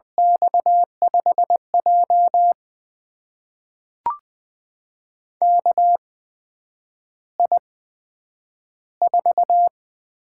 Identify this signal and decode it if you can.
Morse code, heard as a keyed tone, about 20 wpm, spaced as for 9 wpm, 700 Hz. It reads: X5J KI4
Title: USA Call Signs - 4 Characters - 8x Character Spacing - 20wpm